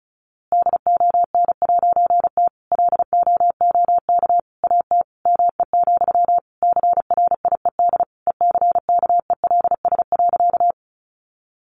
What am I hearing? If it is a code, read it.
Morse code, heard as a keyed tone, about 34 wpm, 700 Hz. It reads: DON'T LOOK AT ME, CRIED ECKELS.
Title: A Sound Of Thunder 35wpm